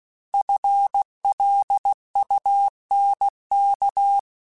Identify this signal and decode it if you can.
Morse code, heard as a keyed tone, about 16 wpm, 800 Hz. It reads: FLUNK
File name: FL Runs / FL Runs mp3s / FL Runs @15